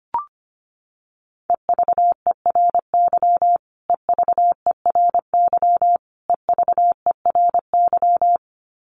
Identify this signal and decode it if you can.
Morse code, heard as a keyed tone, about 25 wpm, 700 Hz. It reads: EVERY EVERY EVERY
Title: Top 500 Words Repeated 25wpm